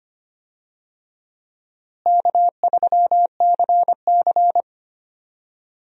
Morse code, heard as a keyed tone, about 25 wpm, 700 Hz. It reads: K3CC
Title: USA Call Signs - 4 Characters - Repeated 2x - 25wpm